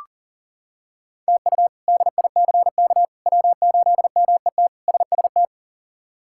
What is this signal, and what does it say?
Morse code, heard as a keyed tone, about 40 wpm, 700 Hz. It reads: TU DICK W8MET SST
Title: K1USN SST QSO Practice - 40wpm